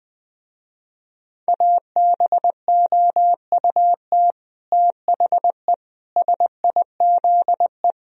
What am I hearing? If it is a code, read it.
Morse code, heard as a keyed tone, about 22 wpm, 700 Hz. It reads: ABOUT THE SIZE
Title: Speed-Racing - Sets of 5 Words 20wpm